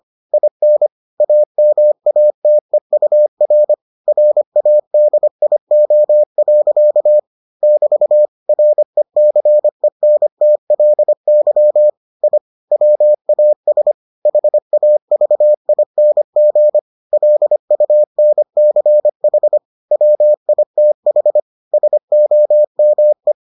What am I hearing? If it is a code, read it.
Morse code, heard as a keyed tone, about 25 wpm, 600 Hz. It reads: IN AMATEUR RADIO. <BT> RECENTLY I WAS HAVING LUNCH WITH SOME